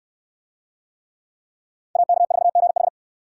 Morse code, heard as a keyed tone, about 60 wpm, 700 Hz. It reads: AK4GH